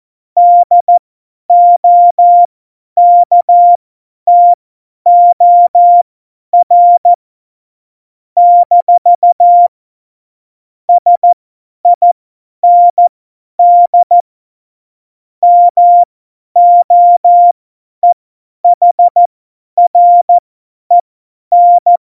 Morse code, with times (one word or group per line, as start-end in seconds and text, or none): 0.4-7.1 DOKTOR
8.4-9.7 -
10.9-14.2 SIND
15.4-22.0 MOEHREN